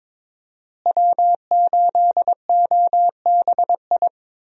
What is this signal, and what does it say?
Morse code, heard as a keyed tone, about 22 wpm, 700 Hz. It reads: W8OBI
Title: USA Call Signs - 5 Characters - Repeated 2x - 22wpm